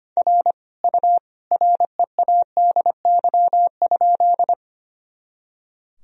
Morse code, heard as a keyed tone, about 25 wpm, 700 Hz. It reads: R U READY?
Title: Sentences from Top 400 Words and Abbreviations 25wpm